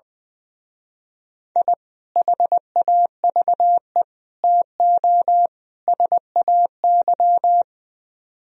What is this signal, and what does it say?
Morse code, heard as a keyed tone, about 22 wpm, 700 Hz. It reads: I HAVE TO SAY
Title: Speed-Racing - Sets of 4 Words 20wpm